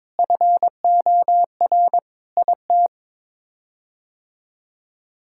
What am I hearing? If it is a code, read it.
Morse code, heard as a keyed tone, about 22 wpm, 700 Hz. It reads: FOR IT
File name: practice / 22wpm / Sets of 7 Words - 22wpm